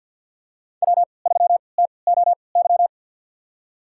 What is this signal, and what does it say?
Morse code, heard as a keyed tone, about 53 wpm, spaced as for 29 wpm, 700 Hz. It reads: W3TOY
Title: USA Call Signs - 5 Characters - 3x Character Spacing - 50wpm